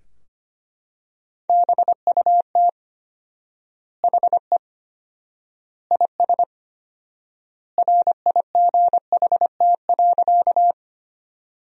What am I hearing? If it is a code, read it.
Morse code, heard as a keyed tone, about 25 wpm, 700 Hz. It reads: BUT HE IS RIGHT.
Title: 4x Word-Spacing - Sentences from Top 200 Words - 25wpm